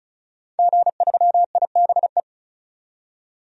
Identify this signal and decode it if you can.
Morse code, heard as a keyed tone, about 34 wpm, 700 Hz. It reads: G3IBE
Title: DXCC-100 Call Signs - 5 Characters - 35wpm